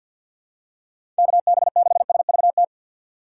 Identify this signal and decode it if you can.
Morse code, heard as a keyed tone, about 47 wpm, 700 Hz. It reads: KB6SVT
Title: USA Call Signs - 6 Characters - 6x Character Spacing - 50wpm